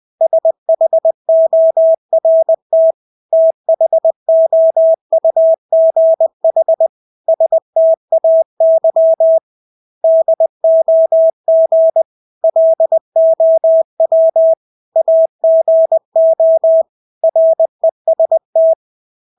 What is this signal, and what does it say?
Morse code, heard as a keyed tone, about 20 wpm, 650 Hz. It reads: SHORT THOUGH STAY DOG LOW AGO REST
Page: Engelske ord 20 wpm